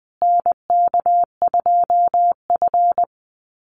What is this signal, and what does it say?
Morse code, heard as a keyed tone, about 20 wpm, 700 Hz. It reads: NK2F
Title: Call Signs Repeated 3x - 20wpm